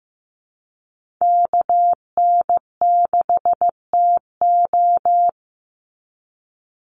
Morse code, heard as a keyed tone, about 15 wpm, 700 Hz. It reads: KN6TO
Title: USA Call Signs 15wpm